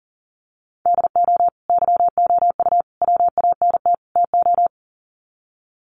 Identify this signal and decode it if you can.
Morse code, heard as a keyed tone, about 40 wpm, 700 Hz. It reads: DO YOU WANT TO
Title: Sets of 4 Words 40wpm